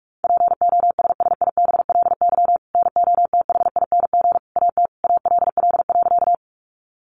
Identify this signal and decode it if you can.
Morse code, heard as a keyed tone, about 44 wpm, 700 Hz. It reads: POSSIBLY NOTHING AT ALL.
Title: Sentences from Top 1300 Words 45wpm